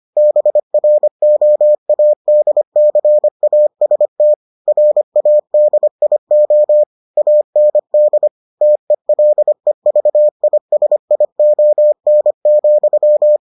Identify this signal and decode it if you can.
Morse code, heard as a keyed tone, about 25 wpm, 600 Hz. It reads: BROADCAST RADIO AND TELEVISION,